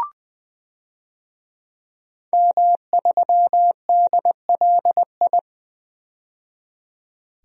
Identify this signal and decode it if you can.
Morse code, heard as a keyed tone, about 21 wpm, 700 Hz. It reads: M3DLI
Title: DXCC-100 Call Signs - 5 Characters - 20wpm